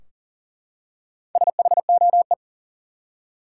Morse code, heard as a keyed tone, about 42 wpm, 700 Hz. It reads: SHOE
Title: Four Letter Words - 40wpm